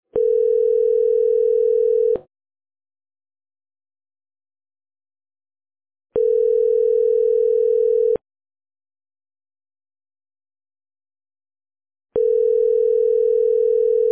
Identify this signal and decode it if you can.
Ringback tone (North America), 440 + 480 Hz.